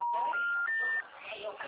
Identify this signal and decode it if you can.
Special information tone, with speech.
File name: out- -2010- - - .4176